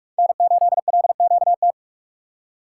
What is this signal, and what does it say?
Morse code, heard as a keyed tone, about 41 wpm, 700 Hz. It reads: N8LQT
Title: USA Call Signs - 5 Characters - 2x Character Spacing - 45wpm